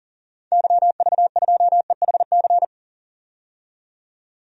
Morse code, heard as a keyed tone, about 40 wpm, 700 Hz. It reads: YV2EHC
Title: DXCC-100 Call Signs - 6 Characters - 40wpm